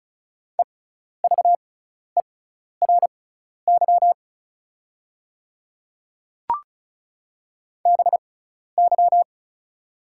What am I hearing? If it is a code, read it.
Morse code, heard as a keyed tone, about 36 wpm, spaced as for 12 wpm, 690 Hz. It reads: EVERY BY
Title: 6x Character Spacing - Top 300 Words - 35wpm